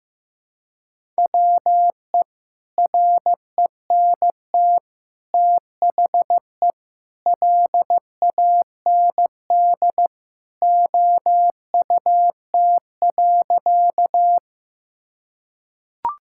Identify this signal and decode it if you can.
Morse code, heard as a keyed tone, about 15 wpm, 700 Hz. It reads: WE RENT THE LAND OUT.
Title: Sentences from Top 1000 Words - Review Part 5 - 15wpm